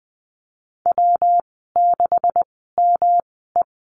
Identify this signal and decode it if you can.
Morse code, heard as a keyed tone, about 20 wpm, spaced as for 15 wpm, 700 Hz. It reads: W6ME
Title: 140 Lesson 23g - Letter f - Call Signs 20wpm